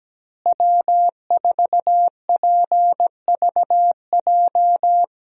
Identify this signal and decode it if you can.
Morse code, heard as a keyed tone, about 17 wpm, 700 Hz. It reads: W4PVJ
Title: USA Call Signs - 5 Characters - Repeated 2x - 17wpm